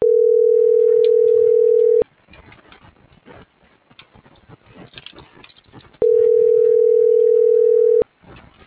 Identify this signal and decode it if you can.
Ringback tone (North America), 440 + 480 Hz.